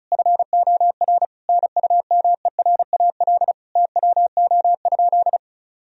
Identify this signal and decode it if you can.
Morse code, heard as a keyed tone, about 36 wpm, 700 Hz. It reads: FOR NUMERAL TWO?
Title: Sentences from Top 1000 Words - Review Part 2 - 35wpm